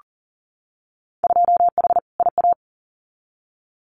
Morse code, heard as a keyed tone, about 40 wpm, 700 Hz. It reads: 2H IA
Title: Winter Field Day Exchanges Encore 40wpm